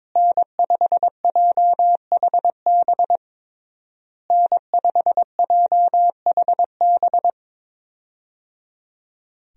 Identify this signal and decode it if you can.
Morse code, heard as a keyed tone, about 22 wpm, 700 Hz. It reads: N5JHB N5JHB
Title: USA Call Signs - 5 Characters - Repeated 3x - 22wpm